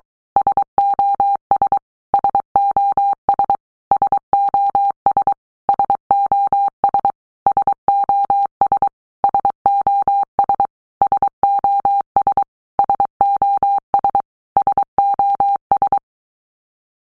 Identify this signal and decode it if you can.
Morse code, heard as a keyed tone, about 23 wpm, 800 Hz. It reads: SOS SOS SOS SOS SOS SOS SOS SOS SOS
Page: SOS sound in Morse Code (Fast)